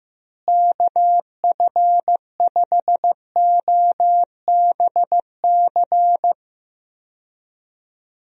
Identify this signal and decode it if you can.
Morse code, heard as a keyed tone, about 15 wpm, 700 Hz. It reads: KF5OBC
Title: USA Call Signs - 6 Characters - Repeated 2x - 15wpm